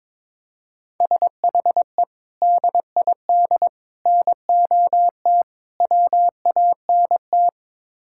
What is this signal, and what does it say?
Morse code, heard as a keyed tone, about 22 wpm, 700 Hz. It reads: SHE DID NOT WANT
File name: practice / 22wpm / Build A Phrase - 5-Word Phrases - 22wpm